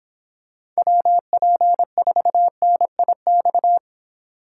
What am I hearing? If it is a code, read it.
Morse code, heard as a keyed tone, about 26 wpm, 700 Hz. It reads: WP4NIX